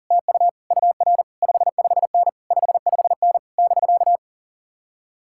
Morse code, heard as a keyed tone, about 52 wpm, 700 Hz. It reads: TU UR 55N 55N <BK>